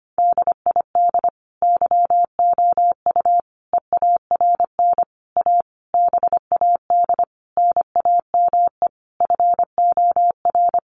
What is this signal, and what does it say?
Morse code, heard as a keyed tone, about 25 wpm, 700 Hz. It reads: DID YOU EARN A BAD NAME FOR
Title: Sentences from Top 1500 Words 25wpm